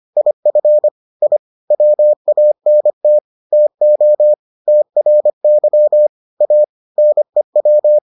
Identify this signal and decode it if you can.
Morse code, heard as a keyed tone, about 25 wpm, 600 Hz. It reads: IF I WANT TO TRY A NEW